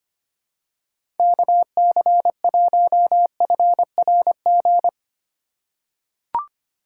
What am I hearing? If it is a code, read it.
Morse code, heard as a keyed tone, about 25 wpm, 700 Hz. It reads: KC1FRG
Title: USA Call Signs - 6 Characters - 3x Character Spacing - 25wpm